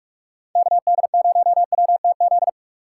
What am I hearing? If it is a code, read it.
Morse code, heard as a keyed tone, about 46 wpm, 700 Hz. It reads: KD0WTZ
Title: USA Call Signs - 6 Characters - 6x Character Spacing - 45wpm